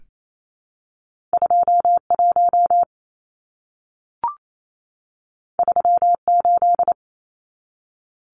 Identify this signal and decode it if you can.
Morse code, heard as a keyed tone, about 28 wpm, 700 Hz. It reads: 21 38
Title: Numbers - 2-Digit 28wpm